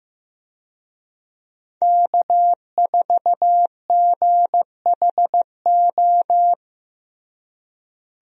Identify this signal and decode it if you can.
Morse code, heard as a keyed tone, about 15 wpm, 700 Hz. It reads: K4GHO
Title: USA Call Signs - 5 Characters - Repeated 2x - 15wpm